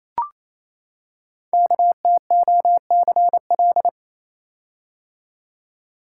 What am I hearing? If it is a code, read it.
Morse code, heard as a keyed tone, about 27 wpm, 700 Hz. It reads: KTOCL
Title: Code Groups - 5 Letters - 28wpm